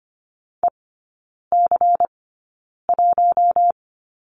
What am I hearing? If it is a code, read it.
Morse code, heard as a keyed tone, about 25 wpm, spaced as for 9 wpm, 700 Hz. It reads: EC1
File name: course / 25:10wpm / 093 Lesson 16e - Letter c - Three Characters 25wpm